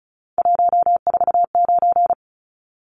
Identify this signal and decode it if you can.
Morse code, heard as a keyed tone, about 34 wpm, 700 Hz. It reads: 149
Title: Numbers - 3-Digit 35wpm